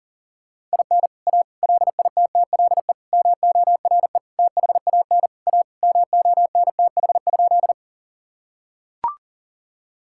Morse code, heard as a keyed tone, about 37 wpm, 700 Hz. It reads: IN A LITTLE MORE THAN A MONTH?
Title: Sentences from Top 1000 Words - Review Part 5 - 40wpm